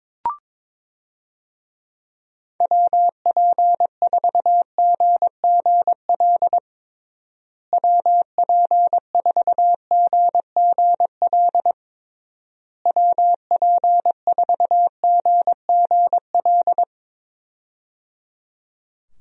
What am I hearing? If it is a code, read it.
Morse code, heard as a keyed tone, about 22 wpm, 700 Hz. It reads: WP4GGL WP4GGL WP4GGL
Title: USA Call Signs - 6 Characters - Repeated 3x - 22wpm